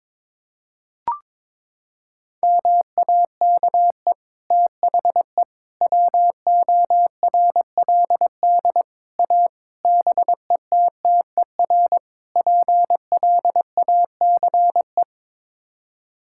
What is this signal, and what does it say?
Morse code, heard as a keyed tone, about 22 wpm, 700 Hz. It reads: MAKE THE WORLD A BETTER PLACE
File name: practice / 22wpm / Sets of 6 Words - 22wpm